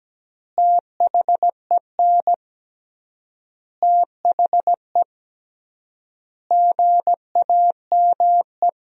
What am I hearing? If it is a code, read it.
Morse code, heard as a keyed tone, about 17 wpm, 700 Hz. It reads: THEN THE GAME
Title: 3x Word-Spacing - Sentences from Top 500 Words - 17wpm